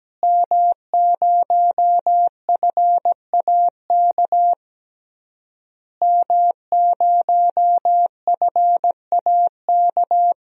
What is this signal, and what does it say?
Morse code, heard as a keyed tone, about 17 wpm, 700 Hz. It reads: M0FAK M0FAK
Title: DXCC-100 Call Signs - 5 Characters - Repeated 3x - 17wpm